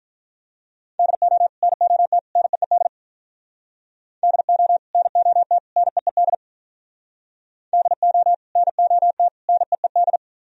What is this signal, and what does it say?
Morse code, heard as a keyed tone, about 42 wpm, 700 Hz. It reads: DO NOT NEED DO NOT NEED DO NOT NEED
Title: Speed-Racing - Sets of 3 Words 35wpm